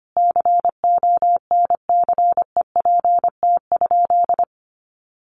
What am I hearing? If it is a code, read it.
Morse code, heard as a keyed tone, about 25 wpm, 700 Hz. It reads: CONCEPT?
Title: Sentences from Top 1100 Words 25wpm